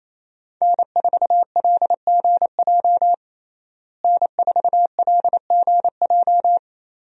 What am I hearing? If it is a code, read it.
Morse code, heard as a keyed tone, about 27 wpm, 700 Hz. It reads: N4LGJ N4LGJ
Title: USA Call Signs - 5 Characters - Repeated 3x - 28wpm